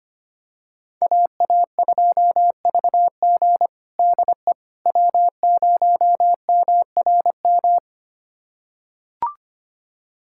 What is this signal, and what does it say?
Morse code, heard as a keyed tone, about 25 wpm, 700 Hz. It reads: AA2VG DE W0MRM
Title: USA Callsign DE Callsign - 5 Characters - 25wpm